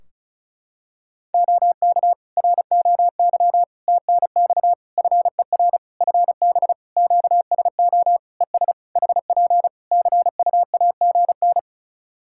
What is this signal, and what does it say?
Morse code, heard as a keyed tone, about 34 wpm, 700 Hz. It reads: OK ROY TNX FER FB QSO ES HP CUAGN
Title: Practice QSO - Exchange 4 - Ending - 35wpm